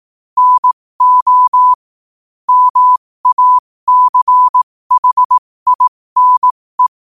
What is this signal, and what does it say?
Morse code, heard as a keyed tone, about 18 wpm, spaced as for 15 wpm, 1000 Hz. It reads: NO MACHINE
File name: qotdfiles / 15wpm